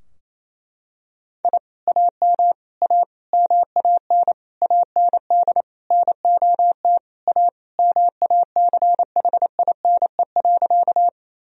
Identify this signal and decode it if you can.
Morse code, heard as a keyed tone, about 27 wpm, 700 Hz. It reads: I AM A MAN AND NOT A MACHINE.
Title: Sentences from Top 1000 Words - Review Part 2 - 28wpm